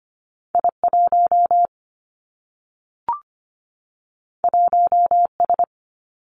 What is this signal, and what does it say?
Morse code, heard as a keyed tone, about 25 wpm, 700 Hz. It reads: I 1 1 S
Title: Two Challenging Letter-Number Combinations 25wpm